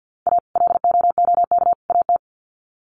Morse code, heard as a keyed tone, about 50 wpm, 700 Hz. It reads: A LOOK AT
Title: Sets of 3 Words 50wpm